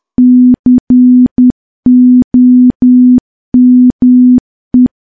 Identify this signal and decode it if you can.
Morse code, heard as a keyed tone, about 10 wpm, 250 Hz. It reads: COME
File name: morse / morse-training-10WPM